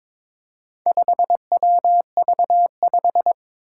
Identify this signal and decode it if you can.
Morse code, heard as a keyed tone, about 22 wpm, 700 Hz. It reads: 5WV5
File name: practice / 22wpm / Four Letter-Number Combinations - 22wpm